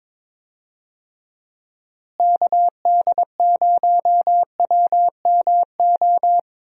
Morse code, heard as a keyed tone, about 22 wpm, 700 Hz. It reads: KD0WMO
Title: USA Call Signs - 6 Characters - Repeated 3x - 22wpm